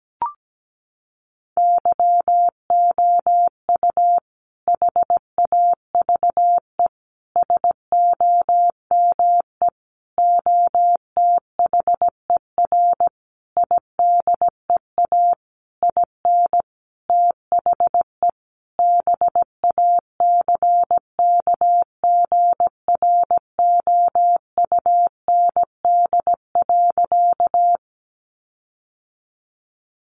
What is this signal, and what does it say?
Morse code, heard as a keyed tone, about 17 wpm, 700 Hz. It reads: YOU HAVE SOME OTHER IDEA IN THE BACKGROUND.
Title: Sentences from Top 1300 Words 17wpm